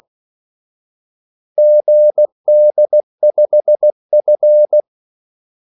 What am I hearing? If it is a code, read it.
Morse code, heard as a keyed tone, about 16 wpm, 600 Hz. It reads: GD5F
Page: Kaldesignaler 16 wpm